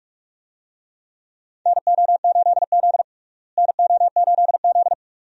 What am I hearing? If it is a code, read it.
Morse code, heard as a keyed tone, about 44 wpm, 700 Hz. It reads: NO8Z NO8Z
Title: USA Call Signs - 4 Characters - Repeated 2x - 45wpm